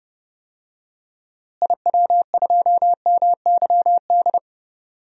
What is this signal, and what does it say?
Morse code, heard as a keyed tone, about 32 wpm, 700 Hz. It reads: IW2MYD
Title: DXCC-100 Call Signs - 6 Characters - Repeated 2x - 30wpm